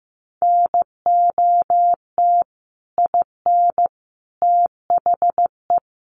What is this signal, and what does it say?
Morse code, heard as a keyed tone, about 15 wpm, 700 Hz. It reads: NOT IN THE
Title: Sentences from Top 700 Words 15wpm